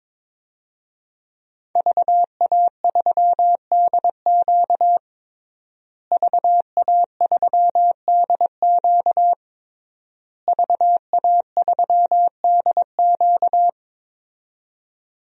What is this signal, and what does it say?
Morse code, heard as a keyed tone, about 22 wpm, 700 Hz. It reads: VA3DQ VA3DQ VA3DQ
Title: DXCC-100 Call Signs - 5 Characters - Repeated 3x - 22wpm